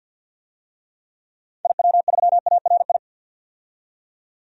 Morse code, heard as a keyed tone, about 50 wpm, 700 Hz. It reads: IW3ARI
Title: DXCC-100 Call Signs - 6 Characters - 50wpm